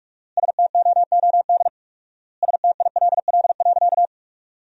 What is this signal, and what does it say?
Morse code, heard as a keyed tone, about 46 wpm, 700 Hz. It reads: STOOD STILL.
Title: 4x Word-Spacing - Sentences from Top 500 Words - 45wpm